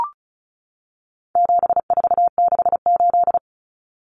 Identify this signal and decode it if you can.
Morse code, heard as a keyed tone, about 34 wpm, 700 Hz. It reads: 7468
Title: Numbers - 4-Digit 35wpm